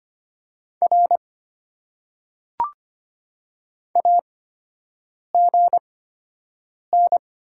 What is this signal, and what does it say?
Morse code, heard as a keyed tone, about 25 wpm, spaced as for 7 wpm, 690 Hz. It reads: R AGN
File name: Top 90 QSO Elements - 8x Character Spacing - 25wpm